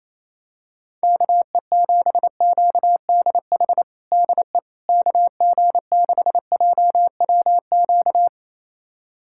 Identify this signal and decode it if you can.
Morse code, heard as a keyed tone, about 27 wpm, 700 Hz. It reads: KE7QDH DE KG6JWQ
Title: USA Callsign DE Callsign - 6 Characters - 28wpm